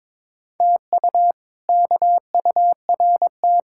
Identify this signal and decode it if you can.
Morse code, heard as a keyed tone, about 22 wpm, 700 Hz. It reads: TU KURT
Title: K1USN SST QSO Practice - 22wpm